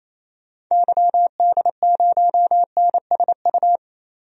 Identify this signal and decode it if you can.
Morse code, heard as a keyed tone, about 28 wpm, 700 Hz. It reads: YD0NSU
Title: DXCC-100 Call Signs - 6 Characters - Repeated 3x - 28wpm